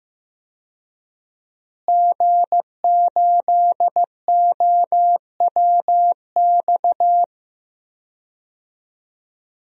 Morse code, heard as a keyed tone, about 15 wpm, 700 Hz. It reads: G8OWX
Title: DXCC-100 Call Signs - 5 Characters - Repeated 3x - 15wpm